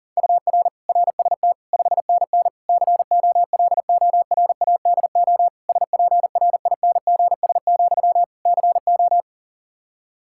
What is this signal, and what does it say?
Morse code, heard as a keyed tone, about 40 wpm, 700 Hz. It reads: UR RST 5NN COLORADO SPRINGS, CO
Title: Practice QSO - Exchange 1 - Beginning - 40wpm